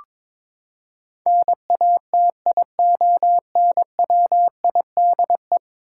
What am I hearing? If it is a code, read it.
Morse code, heard as a keyed tone, about 22 wpm, 700 Hz. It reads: NATIONWIDE
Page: morse-code-ninja-practice-22wpm